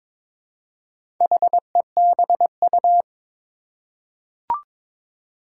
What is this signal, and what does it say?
Morse code, heard as a keyed tone, about 22 wpm, 700 Hz. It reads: HEBU
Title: Four Challenging Letter-Number Combinations 22wpm